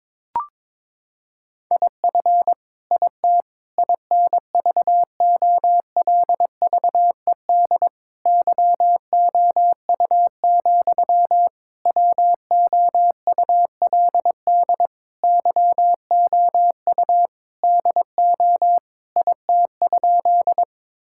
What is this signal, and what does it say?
Morse code, heard as a keyed tone, about 22 wpm, 700 Hz. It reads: IF IT INVOLVED YOU, WOULD YOU DO IT?
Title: Sentences from Top 1000 Words - Review Part 5 - 22wpm